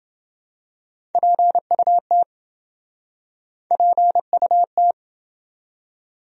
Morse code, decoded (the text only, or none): PUT PUT